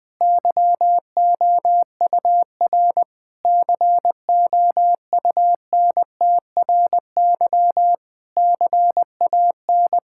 Morse code, heard as a keyed tone, about 20 wpm, 700 Hz. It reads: YOUR COUNTRY CAN
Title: Sentences with 9-Word Phrases - 20wpm